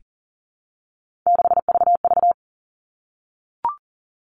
Morse code, heard as a keyed tone, about 40 wpm, 700 Hz. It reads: 6VV
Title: All Cut Numbers - 3-Digit 40wpm